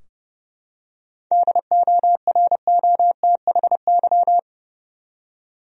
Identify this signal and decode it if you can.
Morse code, heard as a keyed tone, about 30 wpm, 700 Hz. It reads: DOROTHY
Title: USA Common Names - 30wpm